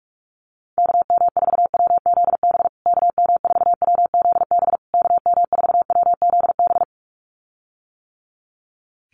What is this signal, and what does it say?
Morse code, heard as a keyed tone, about 44 wpm, 700 Hz. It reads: KM4WZB KM4WZB KM4WZB
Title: USA Call Signs Repeated 3x 45wpm